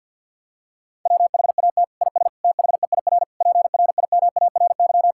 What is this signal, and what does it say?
Morse code, heard as a keyed tone, about 50 wpm, 700 Hz. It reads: WHAT IS THEIR PRIMARY
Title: Sentences from Top 1000 Words - Review Part 5 - 50wpm